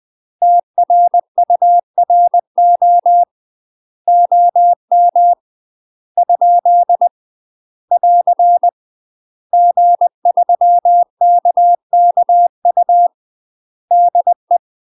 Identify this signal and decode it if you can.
Morse code, heard as a keyed tone, about 20 wpm, 700 Hz. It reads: TRURO OM ? <AR> G3KKU DE
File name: UK MASTER QSOs TRAINING / 20WPM / UK MASTER QSOs 0050